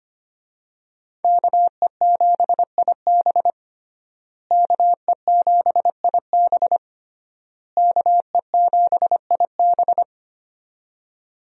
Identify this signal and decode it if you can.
Morse code, heard as a keyed tone, about 25 wpm, 700 Hz. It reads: KE7IB KE7IB KE7IB
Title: USA Call Signs - 5 Characters - Repeated 3x - 25wpm